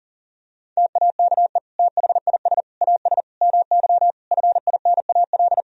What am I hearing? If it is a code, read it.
Morse code, heard as a keyed tone, about 40 wpm, 700 Hz. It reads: TAKE THIS AS MY FINAL
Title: Sentences from Top 1000 Words - Review Part 2 - 40wpm